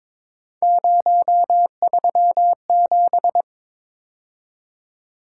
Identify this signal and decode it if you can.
Morse code, heard as a keyed tone, about 25 wpm, 700 Hz. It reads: 037
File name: Speed-Racing - Three Letter-Number Combinations - 22wpm